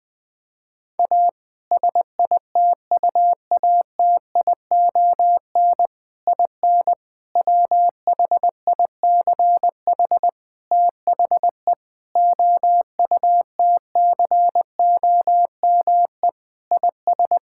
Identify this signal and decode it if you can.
Morse code, heard as a keyed tone, about 21 wpm, 700 Hz. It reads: A SITUATION IN WHICH THE OUTCOME IS
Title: Sentences with 3-Word Phrases - 20wpm